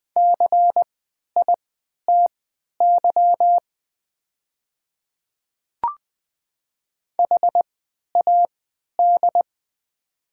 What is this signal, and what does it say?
Morse code, heard as a keyed tone, about 20 wpm, spaced as for 11 wpm, 700 Hz. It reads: CITY HAD
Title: 3x Character Spacing - Top 300 Words - 20wpm